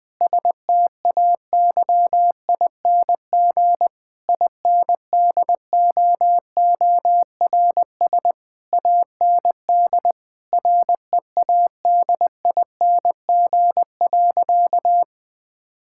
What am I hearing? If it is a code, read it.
Morse code, heard as a keyed tone, about 20 wpm, 700 Hz. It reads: STAYING INDOORS AND READING.
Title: Difficult Sentences with Hints - 20wpm